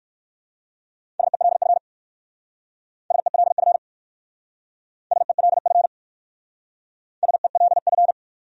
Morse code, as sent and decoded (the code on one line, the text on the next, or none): ... .-.. ..-. / ... .-.. ..-. / ... .-.. ..-. / ... .-.. ..-.
SELF SELF SELF SELF